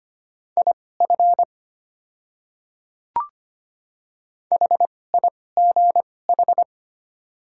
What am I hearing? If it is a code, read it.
Morse code, heard as a keyed tone, about 25 wpm, spaced as for 18 wpm, 700 Hz. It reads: IF HIGH
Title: 2x Character Spacing - Top 200 Words - 25wpm